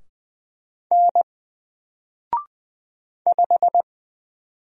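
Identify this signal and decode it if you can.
Morse code, heard as a keyed tone, about 20 wpm, 700 Hz. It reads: N 5
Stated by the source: Single letter and number Morse drill at 20 wpm, from N4LCD.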